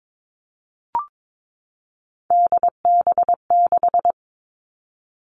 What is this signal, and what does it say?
Morse code, heard as a keyed tone, about 22 wpm, 700 Hz. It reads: DB6
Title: All Cut Numbers - 3-Digit 22wpm